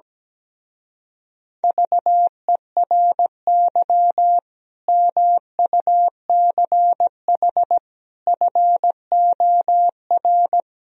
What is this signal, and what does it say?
Morse code, heard as a keyed tone, about 17 wpm, 700 Hz. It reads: VERY MUCH FOR
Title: Speed-Racing - Sets of 5 Words 17wpm